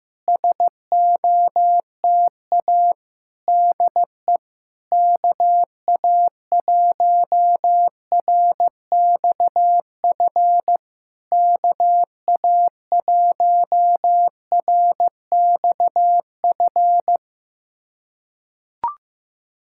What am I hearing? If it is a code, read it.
Morse code, heard as a keyed tone, about 16 wpm, 700 Hz. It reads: SOTA DE KA1RXF KA1RXF
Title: SOTA QSO Practice 15wpm